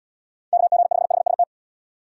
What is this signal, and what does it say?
Morse code, heard as a keyed tone, about 70 wpm, 700 Hz. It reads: KG5HST